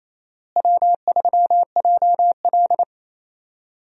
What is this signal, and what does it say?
Morse code, heard as a keyed tone, about 27 wpm, 700 Hz. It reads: W3JL
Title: USA Call Signs - 4 Characters - Repeated 3x - 28wpm